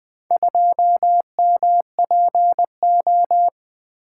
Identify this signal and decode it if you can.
Morse code, heard as a keyed tone, about 20 wpm, 700 Hz. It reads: 2MPO
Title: Four Letter-Number Combinations - 20wpm